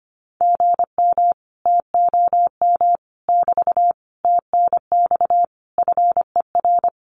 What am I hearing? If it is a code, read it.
Morse code, heard as a keyed tone, about 25 wpm, 700 Hz. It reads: GM TOM <BT> TNX FER
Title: Common QSO Phrases Encore 25wpm